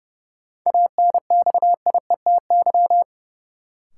Morse code, heard as a keyed tone, about 30 wpm, 700 Hz. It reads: ANXIETY
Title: Sentences from Top 1600 Words - New Word Drill - 30wpm